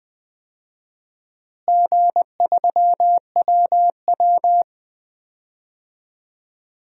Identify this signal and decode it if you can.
Morse code, heard as a keyed tone, about 20 wpm, 700 Hz. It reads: G3WW
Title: DXCC-100 Call Signs - 4 Characters - Repeated 3x - 20wpm